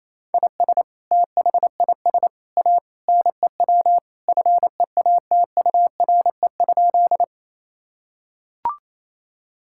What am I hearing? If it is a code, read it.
Morse code, heard as a keyed tone, about 28 wpm, 700 Hz. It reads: IS THIS A NEW FEATURE?
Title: Sentences from Top 1000 Words - Review Part 5 - 28wpm